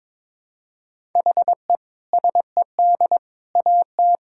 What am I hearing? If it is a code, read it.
Morse code, heard as a keyed tone, about 21 wpm, 700 Hz. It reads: HE SED AT
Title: Sentences from Top 300 Words and Abbreviations 22wpm